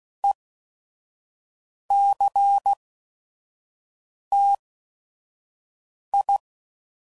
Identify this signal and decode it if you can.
Morse code, heard as a keyed tone, about 16 wpm, spaced as for 5 wpm, 800 Hz. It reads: ECTI
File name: CW Code Groups @5WPM